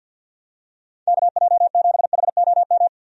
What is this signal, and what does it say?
Morse code, heard as a keyed tone, about 50 wpm, 700 Hz. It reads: KJ7HOM
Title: USA Call Signs - 6 Characters - 50wpm